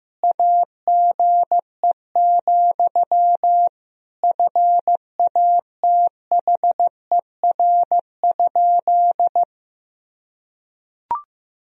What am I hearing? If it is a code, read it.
Morse code, heard as a keyed tone, about 16 wpm, 700 Hz. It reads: AGE, FATHER?
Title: Sentences from Top 1000 Words - Review Part 3 - 15wpm